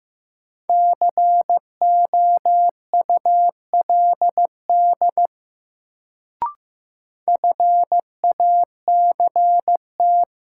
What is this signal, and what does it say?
Morse code, heard as a keyed tone, about 15 wpm, 700 Hz. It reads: COULD FACT
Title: Sets of 2 Words - New Word Drill - 15wpm